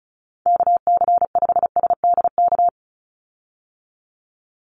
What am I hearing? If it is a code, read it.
Morse code, heard as a keyed tone, about 34 wpm, 700 Hz. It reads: KC5SDK
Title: USA Call Signs 35wpm